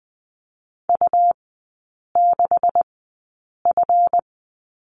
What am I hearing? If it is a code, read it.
Morse code, heard as a keyed tone, about 20 wpm, spaced as for 8 wpm, 700 Hz. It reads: U6F
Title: 138 Lesson 23e - Letter f - Three Characters 20wpm